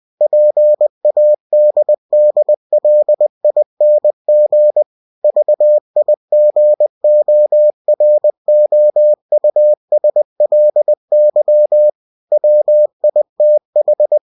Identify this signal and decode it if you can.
Morse code, heard as a keyed tone, about 20 wpm, 600 Hz. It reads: PADDLING VIGOROUSLY WITH
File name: cw-books / winnie-the-pooh / winnie-20wpm-page-0095